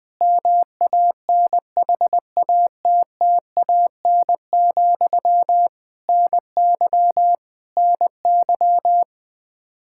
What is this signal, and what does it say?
Morse code, heard as a keyed tone, about 20 wpm, 700 Hz. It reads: MANHATTAN, NY NY
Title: Practice QSO - Exchange 1 - Beginning - 20wpm